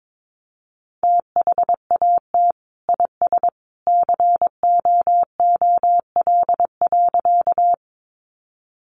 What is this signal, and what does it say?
Morse code, heard as a keyed tone, about 22 wpm, 700 Hz. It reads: THAT IS COOL.
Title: Sentences from Top 1200 Words 22wpm